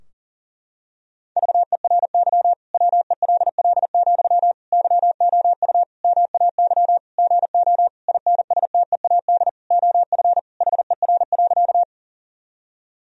Morse code, heard as a keyed tone, about 40 wpm, 700 Hz. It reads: VERY WELL, YOU MAY GO INSTEAD OF HER.
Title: Sentences from Top 1000 Words - Review Part 4 - 40wpm